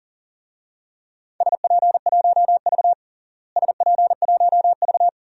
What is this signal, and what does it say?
Morse code, heard as a keyed tone, about 40 wpm, 700 Hz. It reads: SP1V SP1V
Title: DXCC-100 Call Signs - 4 Characters - Repeated 2x - 40wpm